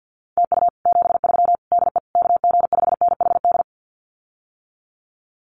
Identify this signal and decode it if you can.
Morse code, heard as a keyed tone, about 48 wpm, 700 Hz. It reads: TU 73 DE KG5NHD
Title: Common QSO Phrases Encore 50wpm